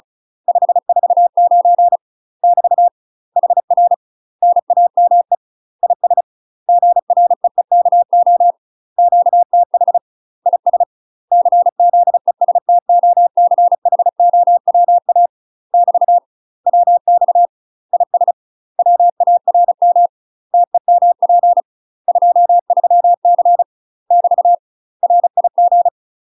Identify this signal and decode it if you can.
Morse code, heard as a keyed tone, about 35 wpm, 700 Hz. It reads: 549 <BT> HR NAME IS GREEKO QTH IS CZESTOCHOWA <BT> WX IS WARM TEMP 23C <BT> RIG
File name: UK MASTER QSOs TRAINING/35WPM